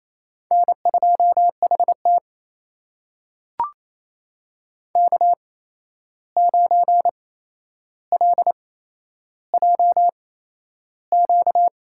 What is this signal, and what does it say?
Morse code, heard as a keyed tone, about 28 wpm, spaced as for 11 wpm, 700 Hz. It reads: N2HT K9LJQ
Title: USA Call Signs - 8x Character Spacing - 28wpm